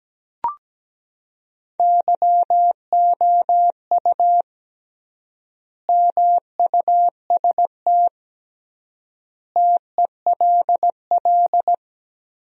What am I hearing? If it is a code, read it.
Morse code, heard as a keyed tone, about 17 wpm, 700 Hz. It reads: YOU MUST TELL
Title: 3x Word-Spacing - Sentences from Top 200 Words - 17wpm